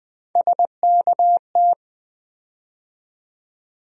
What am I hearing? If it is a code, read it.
Morse code, heard as a keyed tone, about 20 wpm, 700 Hz. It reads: SKT
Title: USA Call Sign Postfixes - 3 Letters 20wpm